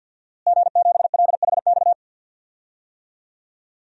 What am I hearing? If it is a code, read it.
Morse code, heard as a keyed tone, about 47 wpm, 700 Hz. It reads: G7LHX
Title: DXCC-100 Call Signs - 5 Characters - Repeated 3x - 50wpm